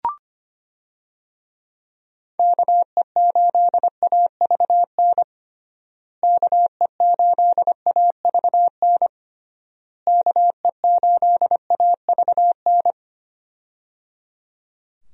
Morse code, heard as a keyed tone, about 25 wpm, 700 Hz. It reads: KE8AVN KE8AVN KE8AVN
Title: USA Call Signs - 6 Characters - Repeated 3x - 25wpm